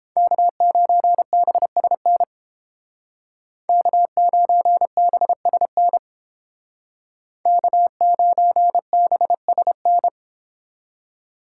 Morse code, heard as a keyed tone, about 30 wpm, 700 Hz. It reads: K9BSN K9BSN K9BSN